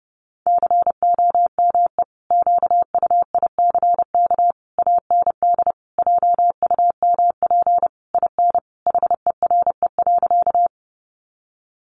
Morse code, heard as a keyed tone, about 30 wpm, 700 Hz. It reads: COME QUICK AND JUMP IN HERE.
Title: Sentences from Top 1300 Words 30wpm